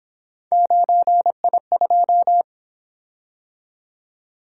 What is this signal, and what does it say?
Morse code, heard as a keyed tone, about 27 wpm, 700 Hz. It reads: 9I2
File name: Speed-Racing - Three Letter-Number Combinations - 22wpm